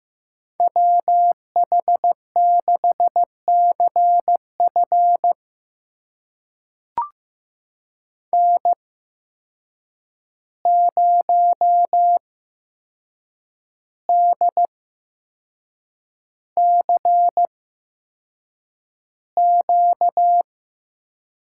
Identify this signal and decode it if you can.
Morse code, heard as a keyed tone, about 15 wpm, spaced as for 6 wpm, 700 Hz. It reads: WH6CF N0DCQ
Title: USA Call Signs - 5 Characters - 8x Character Spacing - 15wpm